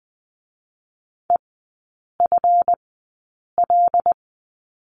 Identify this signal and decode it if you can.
Morse code, heard as a keyed tone, about 20 wpm, spaced as for 8 wpm, 690 Hz. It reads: EFL
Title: 212 Lesson 34e - Number 8 - Three Characters 20wpm